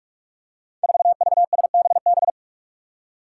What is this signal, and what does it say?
Morse code, heard as a keyed tone, about 44 wpm, 700 Hz. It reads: 4VSBB
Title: Five Challenging Letter-Number Combinations 45wpm